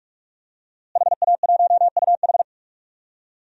Morse code, heard as a keyed tone, about 41 wpm, 700 Hz. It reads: HA1UH
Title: DXCC-100 Call Signs - 5 Characters - Repeated 3x - 45wpm